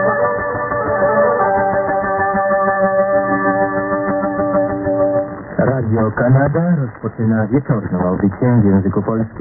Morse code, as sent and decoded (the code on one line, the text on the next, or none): none